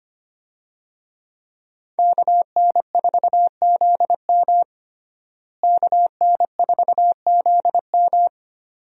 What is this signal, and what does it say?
Morse code, heard as a keyed tone, about 25 wpm, 700 Hz. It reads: KN4ZM KN4ZM
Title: USA Call Signs - 5 Characters - Repeated 2x - 25wpm